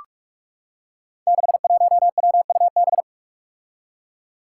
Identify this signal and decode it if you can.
Morse code, heard as a keyed tone, about 46 wpm, 700 Hz. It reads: 61WUB
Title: Five Challenging Letter-Number Combinations 45wpm